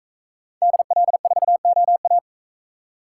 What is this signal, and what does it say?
Morse code, heard as a keyed tone, about 44 wpm, 700 Hz. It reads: DL4OA